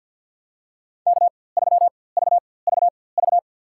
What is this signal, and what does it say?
Morse code, heard as a keyed tone, about 51 wpm, spaced as for 23 wpm, 700 Hz. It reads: K3VVV
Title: USA Call Signs - 5 Characters - 4x Character Spacing - 50wpm